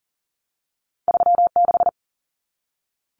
Morse code, heard as a keyed tone, about 40 wpm, 700 Hz. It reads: 36
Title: Common Cut Numbers - 2-Digit 40wpm